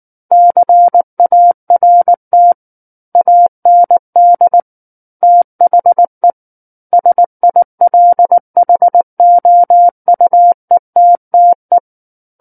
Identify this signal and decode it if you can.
Morse code, heard as a keyed tone, about 19 wpm, 700 Hz. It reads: CART AND THE SILHOUETTE
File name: Book One / 10-Chapter 10 - 19 WPM